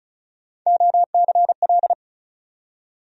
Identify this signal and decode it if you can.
Morse code, heard as a keyed tone, about 36 wpm, 700 Hz. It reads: OCL
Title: USA Call Sign Postfixes - 3 Letters 35wpm